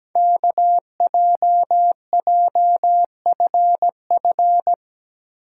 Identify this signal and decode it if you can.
Morse code, heard as a keyed tone, about 17 wpm, 700 Hz. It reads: KJJFF
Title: Code Groups - 5 Letters - 17wpm